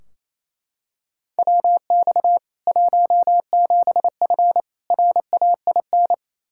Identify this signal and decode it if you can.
Morse code, heard as a keyed tone, about 27 wpm, 700 Hz. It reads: WX 17F RAIN
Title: K1USN SST QSO Practice 28wpm